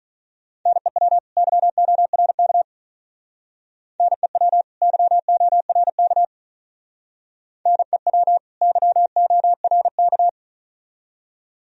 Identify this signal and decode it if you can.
Morse code, heard as a keyed tone, about 41 wpm, 700 Hz. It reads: NEW YORK NEW YORK NEW YORK
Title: Speed-Racing - Sets of 2 Words 35wpm